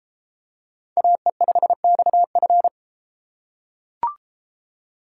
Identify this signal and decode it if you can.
Morse code, heard as a keyed tone, about 32 wpm, 700 Hz. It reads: AE5XF